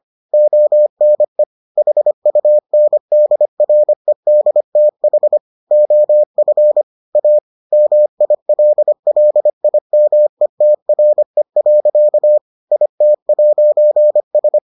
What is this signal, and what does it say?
Morse code, heard as a keyed tone, about 25 wpm, 600 Hz. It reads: ONE HUNDREDTH OF A MILLIMETRE. IT'S